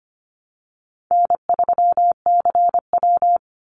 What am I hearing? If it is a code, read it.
Morse code, heard as a keyed tone, about 25 wpm, 700 Hz. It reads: N3CW
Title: Easy Call Signs - 25wpm